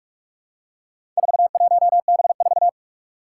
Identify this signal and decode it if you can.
Morse code, heard as a keyed tone, about 44 wpm, 700 Hz. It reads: 41B4
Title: Four Challenging Letter-Number Combinations 45wpm